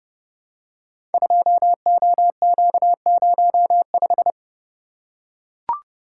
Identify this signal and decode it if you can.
Morse code, heard as a keyed tone, about 30 wpm, 700 Hz. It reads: 2OQ05
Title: Code Groups - 5 Letter:Numbers - 30wpm